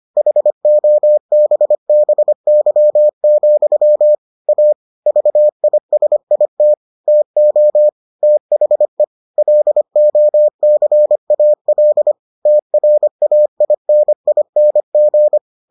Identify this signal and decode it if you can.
Morse code, heard as a keyed tone, about 25 wpm, 600 Hz. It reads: HOBBY, A VISIT TO THE LOCAL TRAINING